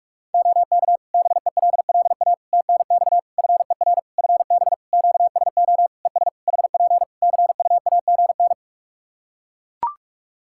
Morse code, heard as a keyed tone, about 41 wpm, 700 Hz. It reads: OK BELLA TNX FER FB QSO ES HP CUAGN
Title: K1USN SST QSO Practice 45wpm